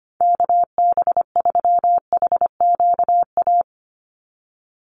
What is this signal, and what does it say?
Morse code, heard as a keyed tone, about 25 wpm, 700 Hz. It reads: KB3HQA
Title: USA Call Signs 25wpm